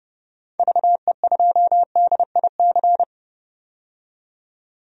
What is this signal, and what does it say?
Morse code, heard as a keyed tone, about 32 wpm, 700 Hz. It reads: VE2DIC
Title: DXCC-100 Call Signs - 6 Characters - 30wpm